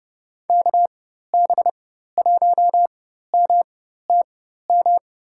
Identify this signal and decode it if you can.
Morse code, heard as a keyed tone, about 32 wpm, spaced as for 14 wpm, 700 Hz. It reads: KB1MTM
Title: USA Call Signs - 6 Characters - 4x Character Spacing - 30wpm